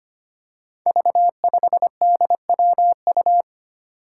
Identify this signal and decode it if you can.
Morse code, heard as a keyed tone, about 25 wpm, 700 Hz. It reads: V5DWU
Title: Five Challenging Letter-Number Combinations 25wpm